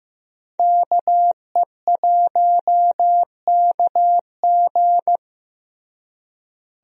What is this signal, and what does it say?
Morse code, heard as a keyed tone, about 15 wpm, 700 Hz. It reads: KE1KG
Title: USA Call Signs - 5 Characters - Repeated 2x - 15wpm